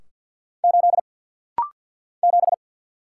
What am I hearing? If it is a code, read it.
Morse code, heard as a keyed tone, about 48 wpm, 700 Hz. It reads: 8 7
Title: Single Letter-Number - Rapid-Fire 50wpm